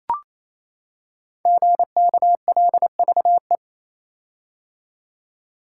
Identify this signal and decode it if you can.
Morse code, heard as a keyed tone, about 28 wpm, 700 Hz. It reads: GKLVE
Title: Code Groups - 5 Letter:Numbers - 28wpm